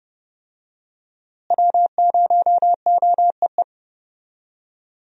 Morse code, heard as a keyed tone, about 30 wpm, 700 Hz. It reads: W0OEE
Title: USA Call Signs - 5 Characters - Repeated 3x - 30wpm